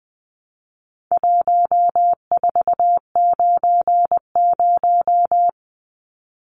Morse code, heard as a keyed tone, about 20 wpm, 700 Hz. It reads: 1490
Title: Numbers - 4-Digit 20wpm